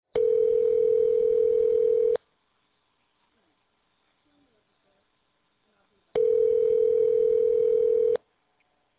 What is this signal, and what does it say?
Ringback tone (North America), 440 + 480 Hz.